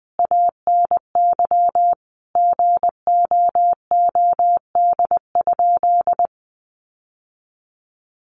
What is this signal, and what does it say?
Morse code, heard as a keyed tone, about 20 wpm, 700 Hz. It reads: ANY GOOD?
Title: Sentences from Top 1200 Words 20wpm